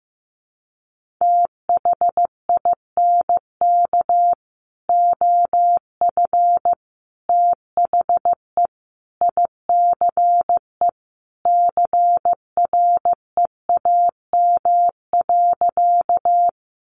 Morse code, heard as a keyed tone, about 15 wpm, 700 Hz. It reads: THINK OF THE ICE CREAM.
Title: Sentences from Top 1500 Words 15wpm